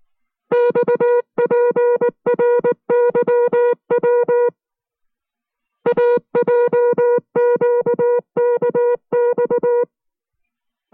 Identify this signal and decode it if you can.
Morse code, heard as a keyed tone, about 19 wpm, 480 Hz. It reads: XPRYW AJQKX